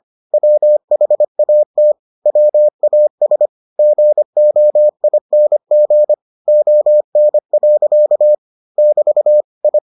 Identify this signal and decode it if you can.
Morse code, heard as a keyed tone, about 25 wpm, 600 Hz. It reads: WHAT WAS GOING ON. <BT> I